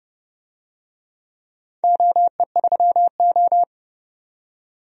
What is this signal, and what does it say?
Morse code, heard as a keyed tone, about 29 wpm, 700 Hz. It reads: OE3O
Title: DXCC-100 Call Signs - 4 Characters - 30wpm